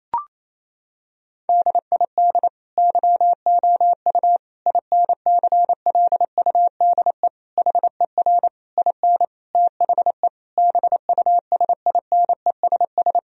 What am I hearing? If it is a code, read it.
Morse code, heard as a keyed tone, about 28 wpm, 700 Hz. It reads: DID YOU INCLUDE HER IN THE BUSINESS
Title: Sentences from Top 1000 Words - Review Part 3 - 28wpm